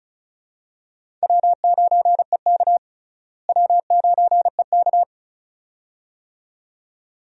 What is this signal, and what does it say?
Morse code, heard as a keyed tone, about 34 wpm, 700 Hz. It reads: W9EK W9EK
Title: USA Call Signs - 4 Characters - Repeated 2x - 35wpm